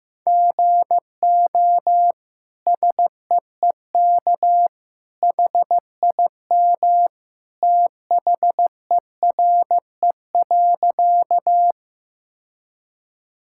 Morse code, heard as a keyed tone, about 16 wpm, 700 Hz. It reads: GO SEEK HIM THERE.
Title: Sentences from Top 1600 Words 15wpm